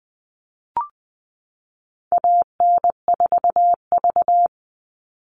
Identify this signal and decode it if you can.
Morse code, heard as a keyed tone, about 20 wpm, 700 Hz. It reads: AN4V
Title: All Cut Numbers - 4-Digit 20wpm